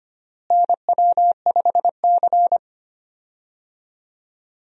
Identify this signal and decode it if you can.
Morse code, heard as a keyed tone, about 25 wpm, 700 Hz. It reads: NW5C
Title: USA Call Signs - 4 Characters - Repeated 3x - 25wpm